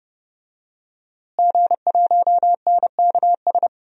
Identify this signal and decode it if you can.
Morse code, heard as a keyed tone, about 30 wpm, 700 Hz. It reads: G1NKS
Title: DXCC-100 Call Signs - 5 Characters - Repeated 3x - 30wpm